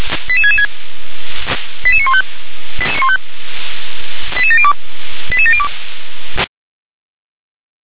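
Five-tone selective call (ZVEI), heard as quick runs of five tones, tones of about 70 ms. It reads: 88666 88915 88702 88702
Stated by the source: SDR reception of ZVEI five-tone call 88707.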